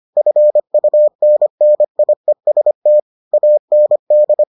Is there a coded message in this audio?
Yes: Morse code from 0.2 to 4.4 s, heard as a keyed tone, about 25 wpm, 600 Hz. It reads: FUNNIEST AND